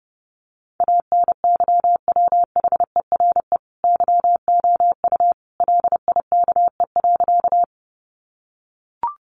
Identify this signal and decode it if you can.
Morse code, heard as a keyed tone, about 30 wpm, 700 Hz. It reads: ANYWHERE YOU LIKE.
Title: Sentences from Top 1400 Words 30wpm